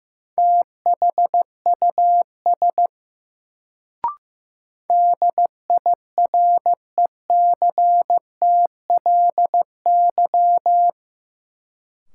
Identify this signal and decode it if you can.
Morse code, heard as a keyed tone, about 15 wpm, 700 Hz. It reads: THUS DIRECTLY
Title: Sentences from Top 800 Words - New Word Drill - 15wpm